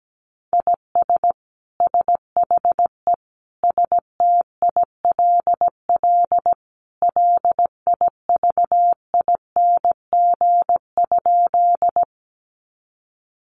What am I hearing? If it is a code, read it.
Morse code, heard as a keyed tone, about 17 wpm, 700 Hz. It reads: IS SHE STILL LIVING?
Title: Sentences from Top 1100 Words 17wpm